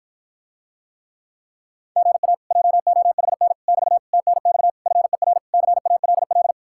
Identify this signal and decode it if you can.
Morse code, heard as a keyed tone, about 54 wpm, 700 Hz. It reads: GA JOHN <BT> TNX FER CALL